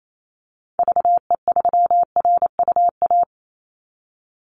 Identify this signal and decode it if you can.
Morse code, heard as a keyed tone, about 28 wpm, 700 Hz. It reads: VE3RUA
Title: Call Signs - 28wpm